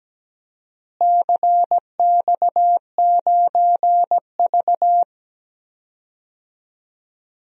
Morse code, heard as a keyed tone, about 17 wpm, 700 Hz. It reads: CX9V
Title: DXCC-100 Call Signs - 4 Characters - Repeated 3x - 17wpm